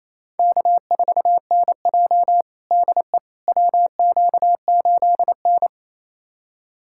Morse code, heard as a keyed tone, about 28 wpm, 700 Hz. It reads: K4NJ DE WQ8N
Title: USA Callsign DE Callsign - 4 Characters - 28wpm